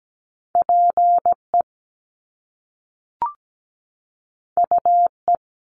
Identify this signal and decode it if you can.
Morse code, heard as a keyed tone, about 17 wpm, 700 Hz. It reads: PE UE
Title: Top Two Letter Combinations in Words 17wpm